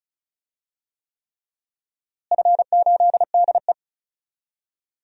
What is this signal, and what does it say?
Morse code, heard as a keyed tone, about 34 wpm, 700 Hz. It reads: F8DE
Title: DXCC-100 Call Signs - 4 Characters - Repeated 3x - 35wpm